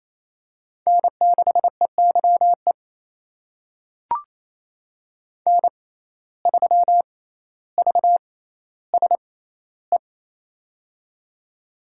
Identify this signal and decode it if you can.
Morse code, heard as a keyed tone, about 29 wpm, spaced as for 15 wpm, 700 Hz. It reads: N6EYE N3VSE
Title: USA Call Signs - 5 Characters - 6x Character Spacing - 28wpm